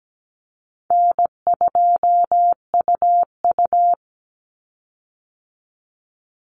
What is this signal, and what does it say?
Morse code, heard as a keyed tone, about 17 wpm, 700 Hz. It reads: N2UU
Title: Easy Call Signs Repeated 3x - 17wpm